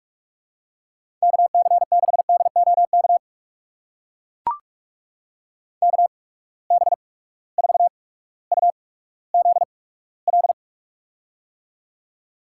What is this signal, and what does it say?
Morse code, heard as a keyed tone, about 42 wpm, spaced as for 19 wpm, 700 Hz. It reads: KC6DOK KB4UZL
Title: USA Call Signs - 6 Characters - 8x Character Spacing - 45wpm